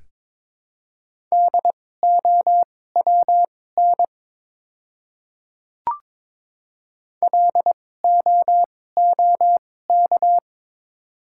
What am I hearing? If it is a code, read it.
Morse code, heard as a keyed tone, about 22 wpm, spaced as for 16 wpm, 700 Hz. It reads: DOWN LOOK
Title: 2x Character Spacing - Top 100 Words - 22wpm